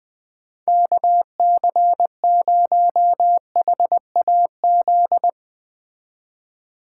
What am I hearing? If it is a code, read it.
Morse code, heard as a keyed tone, about 20 wpm, 700 Hz. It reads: KC0HAZ
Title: Speed Racing - USA 2x3 Call Signs 17wpm